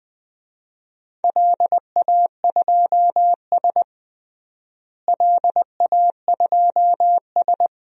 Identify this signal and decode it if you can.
Morse code, heard as a keyed tone, about 21 wpm, 700 Hz. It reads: LA2S LA2S
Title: DXCC-100 Call Signs - 4 Characters - Repeated 2x - 20wpm